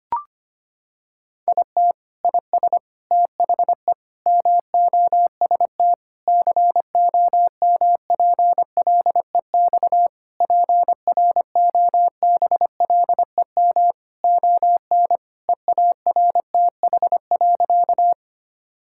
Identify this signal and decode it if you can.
Morse code, heard as a keyed tone, about 25 wpm, 700 Hz. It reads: IT IS THE MOST COMPLEX PROBLEM ON EARTH.
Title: Sentences from Top 1000 Words - Review Part 6 - 25wpm